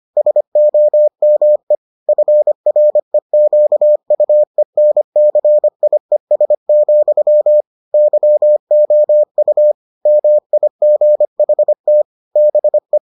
Morse code, heard as a keyed tone, about 25 wpm, 600 Hz. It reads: SOME FREQUENCIES, YOU MIGHT BE